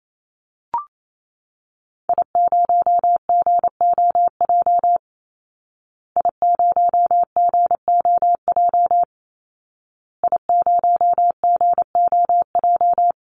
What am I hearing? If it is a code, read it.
Morse code, heard as a keyed tone, about 28 wpm, 700 Hz. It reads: I0GOJ I0GOJ I0GOJ
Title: Call Signs Repeated 2x - 28wpm